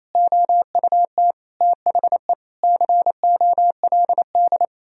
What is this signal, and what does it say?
Morse code, heard as a keyed tone, about 28 wpm, 700 Hz. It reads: OUT THE COLD
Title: Sentences from Top 1000 Words - Review Part 2 - 28wpm